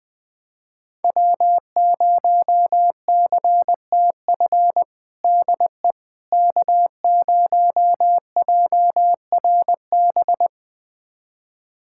Morse code, heard as a keyed tone, about 20 wpm, 700 Hz. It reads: W0CTF DE K0JRB
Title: USA Callsign DE Callsign - 5 Characters - 20wpm